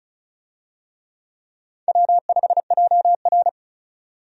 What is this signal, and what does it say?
Morse code, heard as a keyed tone, about 36 wpm, 700 Hz. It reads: W5JR
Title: USA Call Signs - 4 Characters - 35wpm